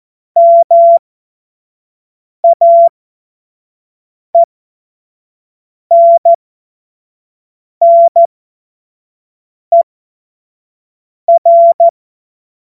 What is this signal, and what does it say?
Morse code, heard as a keyed tone, about 14 wpm, spaced as for 5 wpm, 680 Hz. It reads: MAENNER
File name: Morsekurs / Witze / Witz 03. Mai 2022 25 F70